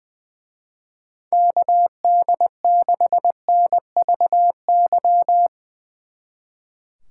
Morse code, heard as a keyed tone, about 21 wpm, 700 Hz. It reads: KD6NVY
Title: Speed Racing - USA 2x3 Call Signs 20wpm